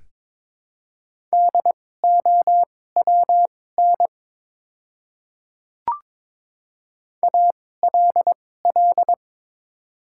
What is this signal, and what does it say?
Morse code, heard as a keyed tone, about 22 wpm, spaced as for 16 wpm, 700 Hz. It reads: DOWN ALL
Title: 2x Character Spacing - Top 100 Words - 22wpm